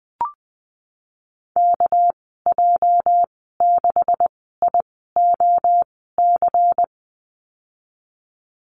Morse code, heard as a keyed tone, about 20 wpm, spaced as for 14 wpm, 700 Hz. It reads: KJ6IOC
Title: 245 Lesson 39g - Letter z - Call Signs 20wpm